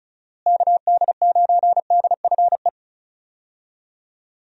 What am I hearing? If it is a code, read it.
Morse code, heard as a keyed tone, about 36 wpm, 700 Hz. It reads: KD9DFE
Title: USA Call Signs - 6 Characters - Repeated 2x - 35wpm